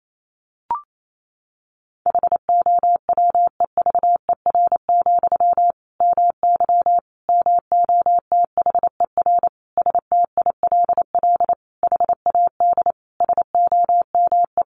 Morse code, heard as a keyed tone, about 28 wpm, 700 Hz. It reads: HOWEVER, MY MOTHER STILL HAD SOME
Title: Sentences from Top 1000 Words 28wpm